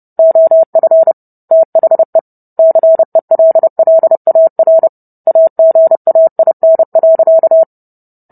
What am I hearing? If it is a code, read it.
Morse code, heard as a keyed tone, about 30 wpm, 650 Hz. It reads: OF THE CELLAR AGAIN.